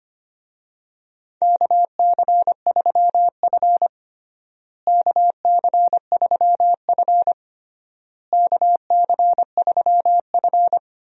Morse code, heard as a keyed tone, about 25 wpm, 700 Hz. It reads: KC3F KC3F KC3F
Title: USA Call Signs - 4 Characters - Repeated 3x - 25wpm